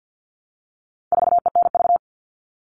Morse code, heard as a keyed tone, about 48 wpm, 700 Hz. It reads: 4ENV
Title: All Cut Numbers - 4-Digit 50wpm